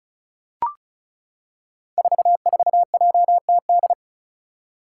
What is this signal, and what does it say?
Morse code, heard as a keyed tone, about 36 wpm, 700 Hz. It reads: 44JTD
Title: Five Challenging Letter-Number Combinations 35wpm